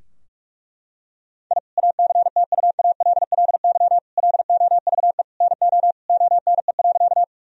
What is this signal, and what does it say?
Morse code, heard as a keyed tone, about 41 wpm, 700 Hz. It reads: I ACTUALLY LOVE NO ONE.
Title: Sentences from Top 1000 Words - Review Part 4 - 45wpm